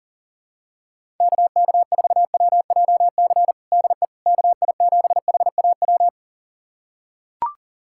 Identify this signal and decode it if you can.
Morse code, heard as a keyed tone, about 40 wpm, 700 Hz. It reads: KK4WJC DE KI7HAW
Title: USA Callsign DE Callsign - 6 Characters - 40wpm